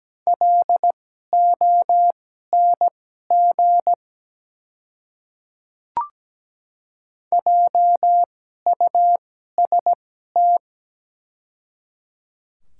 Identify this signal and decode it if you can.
Morse code, heard as a keyed tone, about 17 wpm, spaced as for 12 wpm, 700 Hz. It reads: LONG JUST
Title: 2x Character Spacing - Top 200 Words - 17wpm